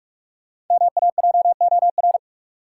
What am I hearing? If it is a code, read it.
Morse code, heard as a keyed tone, about 46 wpm, 700 Hz. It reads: MAJOR
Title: Sentences from Top 600 Words - New Word Drill - 45wpm